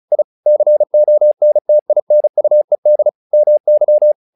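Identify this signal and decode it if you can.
Morse code, heard as a keyed tone, about 34 wpm, 600 Hz. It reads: I CONTINUED MY
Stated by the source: Morse practice text "Tower Trouble" sent at 35 WPM, from the SKCC - Straight Key Century Club learning center audio files.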